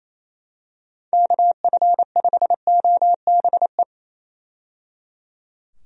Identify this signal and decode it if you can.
Morse code, heard as a keyed tone, about 28 wpm, 700 Hz. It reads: KF5OBE
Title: USA Call Signs - 6 Characters - 28wpm